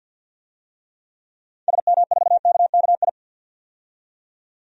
Morse code, heard as a keyed tone, about 46 wpm, 700 Hz. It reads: SM4KKI
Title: DXCC-100 Call Signs - 6 Characters - 50wpm